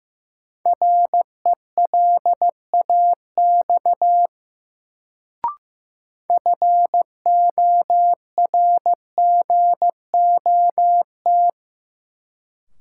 Morse code, heard as a keyed tone, about 15 wpm, 700 Hz. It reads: RELAX FORGOT
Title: Sentences from Top 1400 Words - New Word Drill - 15wpm